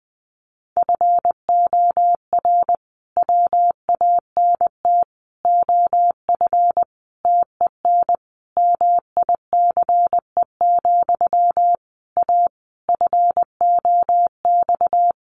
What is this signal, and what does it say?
Morse code, heard as a keyed tone, about 20 wpm, 700 Hz. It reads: FOR WANT OF TEN MICE, A FOX
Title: A Sound Of Thunder 20wpm